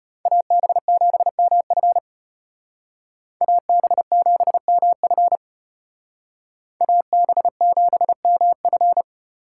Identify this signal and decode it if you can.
Morse code, heard as a keyed tone, about 32 wpm, 700 Hz. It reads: AB7MF AB7MF AB7MF